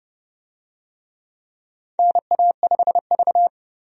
Morse code, heard as a keyed tone, about 32 wpm, 700 Hz. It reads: NA5V
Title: USA Call Signs - 4 Characters - 30wpm